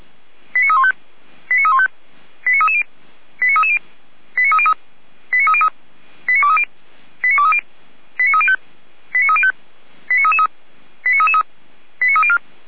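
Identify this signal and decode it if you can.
Five-tone selective call (ZVEI), heard as quick runs of five tones, tones of about 70 ms. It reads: 89315 89315 89330 89330 89303 89303 89230 89230 89305 89305 89333 89333 89304
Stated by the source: Five-tone ZVEI selective call received by SDR.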